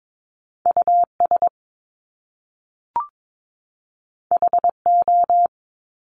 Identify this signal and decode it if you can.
Morse code, heard as a keyed tone, about 22 wpm, 700 Hz. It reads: US HO
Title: Top Two Letter Combinations in Words 22wpm